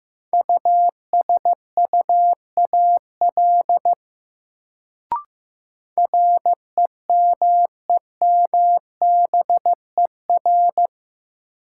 Morse code, heard as a keyed tone, about 15 wpm, 700 Hz. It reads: USUAL REMEMBER
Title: Sentences from Top 400 Words - New Word Drill - 15wpm